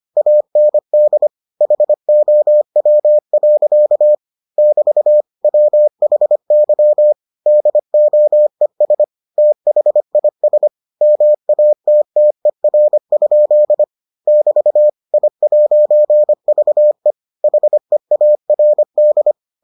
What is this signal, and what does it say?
Morse code, heard as a keyed tone, about 25 wpm, 600 Hz. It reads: AND HOW. <BT> WHY DOES THIS MATTER? <BT> I'VE HEARD